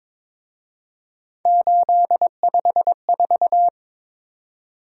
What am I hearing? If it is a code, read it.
Morse code, heard as a keyed tone, about 26 wpm, 700 Hz. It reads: 854
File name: Speed-Racing - Three Letter-Number Combinations - 22wpm